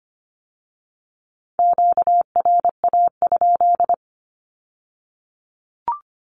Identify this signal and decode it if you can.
Morse code, heard as a keyed tone, about 25 wpm, 700 Hz. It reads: QRA?
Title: Q-Codes with Spoken Meaning - 25wpm